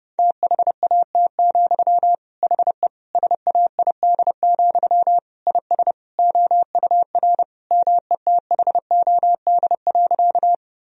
Morse code, heard as a keyed tone, about 30 wpm, 700 Hz. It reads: THAT, HE SAID, IS OUR METHOD.
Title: Sentences from Top 1000 Words - Review Part 3 - 30wpm